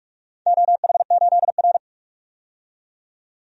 Morse code, heard as a keyed tone, about 44 wpm, 700 Hz. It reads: OH8R
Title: DXCC-100 Call Signs - 4 Characters - 45wpm